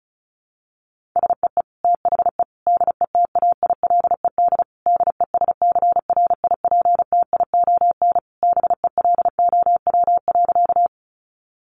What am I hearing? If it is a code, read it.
Morse code, heard as a keyed tone, about 34 wpm, 700 Hz. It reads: SEE THE DETAILED DESCRIPTION BELOW.
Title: Sentences from Top 1500 Words 35wpm